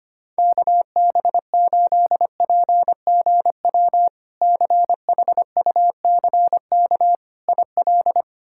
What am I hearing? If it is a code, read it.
Morse code, heard as a keyed tone, about 25 wpm, 700 Hz. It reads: KB8PGW CHUCK IL
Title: K1USN SST QSO Practice - 25wpm